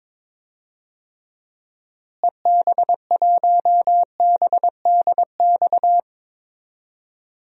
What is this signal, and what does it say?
Morse code, heard as a keyed tone, about 22 wpm, 700 Hz. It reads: EB1BDX
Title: DXCC-100 Call Signs - 6 Characters - Repeated 2x - 22wpm